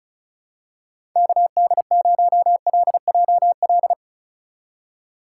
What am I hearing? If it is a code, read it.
Morse code, heard as a keyed tone, about 34 wpm, 700 Hz. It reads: KD0LJL
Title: USA Call Signs - 6 Characters - 4x Character Spacing - 35wpm